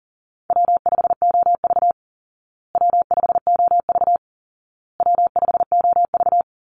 Morse code, heard as a keyed tone, about 40 wpm, 700 Hz. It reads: W5OV W5OV W5OV
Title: Easy Call Signs Repeated 3x - 40wpm